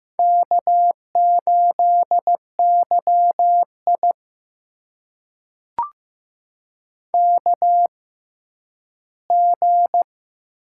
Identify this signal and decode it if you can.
Morse code, heard as a keyed tone, about 15 wpm, spaced as for 10 wpm, 700 Hz. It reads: K8YI KG
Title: USA Call Signs - 6x Character Spacing - 15wpm